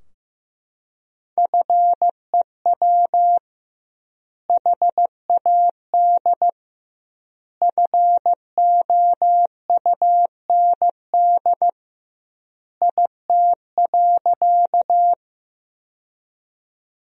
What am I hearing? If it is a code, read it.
Morse code, heard as a keyed tone, about 15 wpm, 700 Hz. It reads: FEW HAD FOUND IT.
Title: 2x Word-Spacing - Sentences from Top 300 Words - 15wpm